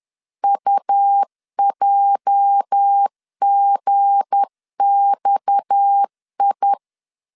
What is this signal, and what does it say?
Morse code, heard as a keyed tone, about 10 wpm, 800 Hz. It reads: UJGXI
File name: LEZIONI / 8 YJ45 / 8 YJ 10wpm 2651